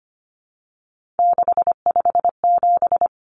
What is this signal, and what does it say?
Morse code, heard as a keyed tone, about 25 wpm, 700 Hz. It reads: 657
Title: Common Cut Numbers - 3-Digit 25wpm